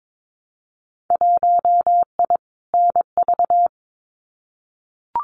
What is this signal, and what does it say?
Morse code, heard as a keyed tone, about 22 wpm, 700 Hz. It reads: 1I NV
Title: Winter Field Day Exchanges Encore 22wpm